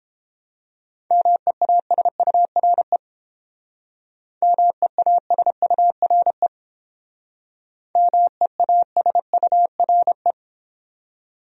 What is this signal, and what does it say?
Morse code, heard as a keyed tone, about 30 wpm, 700 Hz. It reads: MEASURE MEASURE MEASURE